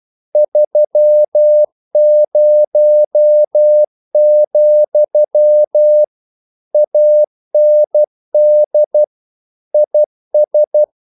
Morse code, heard as a keyed tone, about 12 wpm, 600 Hz. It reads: 30, AND IS
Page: English quotations in CW (telegraphy - morse)